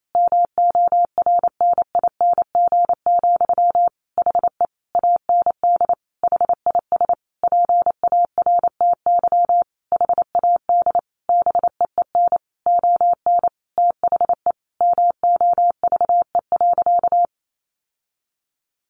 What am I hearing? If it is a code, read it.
Morse code, heard as a keyed tone, about 28 wpm, 700 Hz. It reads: MORNING, HE AND HIS PARTY HAD BEEN ON THE MOVE.
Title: Sentences from Top 1400 Words 28wpm